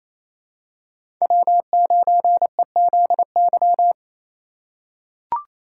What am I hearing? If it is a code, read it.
Morse code, heard as a keyed tone, about 28 wpm, 700 Hz. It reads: W9EZY
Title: USA Call Signs - 5 Characters - 4x Character Spacing - 28wpm